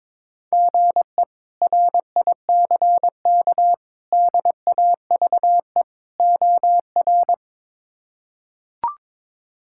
Morse code, heard as a keyed tone, about 22 wpm, 700 Hz. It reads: GE RICK DAVE OR
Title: K1USN SST QSO Practice - 22wpm